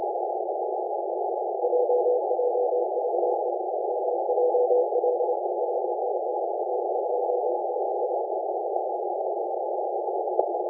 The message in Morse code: none